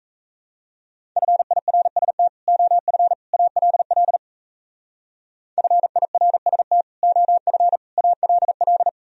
Morse code, heard as a keyed tone, about 34 wpm, 700 Hz. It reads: FIRST OF ALL FIRST OF ALL
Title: Speed-Racing - Sets of 3 Words 28wpm